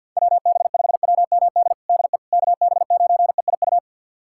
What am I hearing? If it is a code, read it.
Morse code, heard as a keyed tone, about 49 wpm, 700 Hz. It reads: WB5WMD DE KB9EIU
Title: USA Callsign DE Callsign - 6 Characters - 50wpm